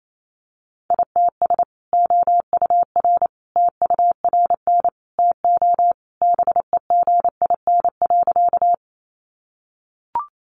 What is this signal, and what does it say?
Morse code, heard as a keyed tone, about 28 wpm, 700 Hz. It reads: ITS OUR TURN TO BEGIN.
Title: Sentences from Top 600 Words 28wpm